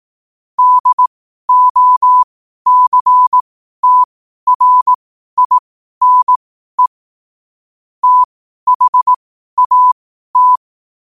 Morse code, heard as a keyed tone, about 18 wpm, spaced as for 12 wpm, 1000 Hz. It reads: DOCTRINE THAT
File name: qotdfiles / 12wpm